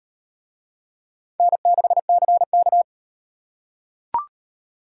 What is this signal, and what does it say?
Morse code, heard as a keyed tone, about 38 wpm, 700 Hz. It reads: N6CK